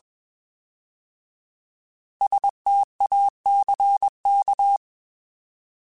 Morse code, heard as a keyed tone, about 21 wpm, 800 Hz. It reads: STACK
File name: Alphabetical / Alphabetical mp3s / Ss @20